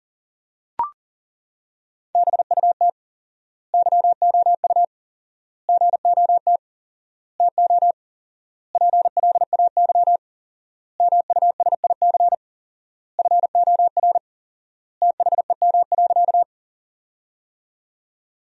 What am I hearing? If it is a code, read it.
Morse code, heard as a keyed tone, about 37 wpm, 700 Hz. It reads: BUT YOU GOT TO PLAY MUSIC FOR THEM.
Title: 4x Word-Spacing - Sentences from Top 300 Words - 40wpm